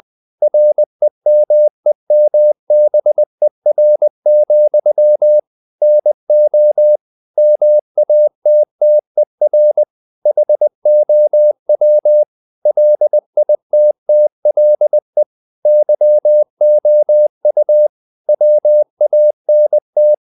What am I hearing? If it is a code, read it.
Morse code, heard as a keyed tone, about 20 wpm, 600 Hz. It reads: REMEMBER, NO MATTER HOW LITTLE YOU WANT
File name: the old man and the sea / the old man and the sea part 1 20wpm